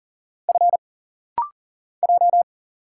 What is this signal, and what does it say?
Morse code, heard as a keyed tone, about 40 wpm, 700 Hz. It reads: F J
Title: Single Letter-Number Alt - Rapid-Fire 40wpm